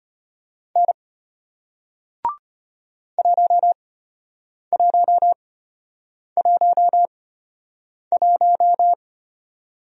Morse code, heard as a keyed tone, about 33 wpm, spaced as for 8 wpm, 700 Hz. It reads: N 1111